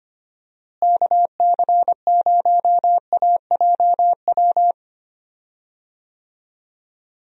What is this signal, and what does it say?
Morse code, heard as a keyed tone, about 25 wpm, 700 Hz. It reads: KC0AJW
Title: USA Call Signs - 6 Characters - Repeated 3x - 25wpm